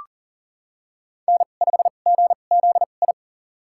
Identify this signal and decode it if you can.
Morse code, heard as a keyed tone, about 40 wpm, spaced as for 29 wpm, 700 Hz. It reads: N5GZI
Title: USA Call Signs - 2x Character Spacing - 40wpm